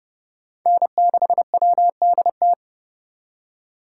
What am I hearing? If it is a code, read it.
Morse code, heard as a keyed tone, about 29 wpm, 700 Hz. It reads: N6WDT